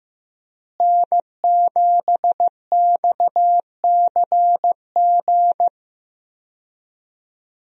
Morse code, heard as a keyed tone, about 15 wpm, 700 Hz. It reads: N7XCG
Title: USA Call Signs - 5 Characters - Repeated 3x - 15wpm